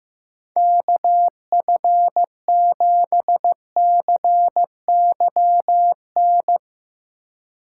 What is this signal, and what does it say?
Morse code, heard as a keyed tone, about 15 wpm, 700 Hz. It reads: KF7CYN
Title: USA Call Signs - 6 Characters - 2x Character Spacing - 15wpm